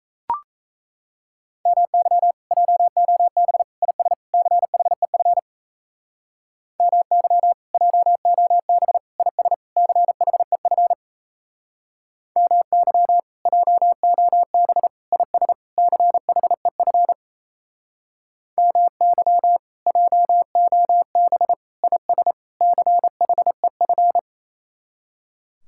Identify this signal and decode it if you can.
Morse code, heard as a keyed tone, about 38 wpm, 700 Hz. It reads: MY JOB IS CHEF MY JOB IS CHEF MY JOB IS CHEF MY JOB IS CHEF